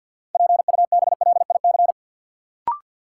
Morse code, heard as a keyed tone, about 50 wpm, 700 Hz. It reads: PUBLIC
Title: Sentences from Top 600 Words - New Word Drill - 50wpm